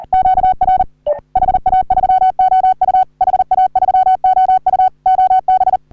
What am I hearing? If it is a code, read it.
Morse code, heard as a keyed tone, about 38 wpm, 750 Hz. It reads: QR HA3OU HA3OU OB
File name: CW / HA3OU RHOBCW REC